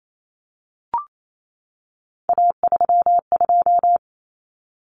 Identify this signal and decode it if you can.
Morse code, heard as a keyed tone, about 28 wpm, 700 Hz. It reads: A32
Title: Common Cut Numbers - 3-Digit 28wpm